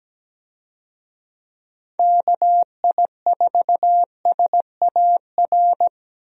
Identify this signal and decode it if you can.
Morse code, heard as a keyed tone, about 17 wpm, 700 Hz. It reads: KI4SAR
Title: USA Call Signs - 6 Characters - Repeated 3x - 17wpm